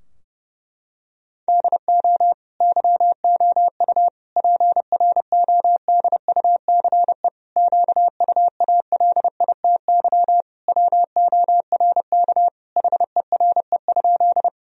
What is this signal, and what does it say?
Morse code, heard as a keyed tone, about 30 wpm, 700 Hz. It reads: DO YOU PRODUCE QUALITY WORK HERE?
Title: Sentences from Top 1000 Words - Review Part 4 - 30wpm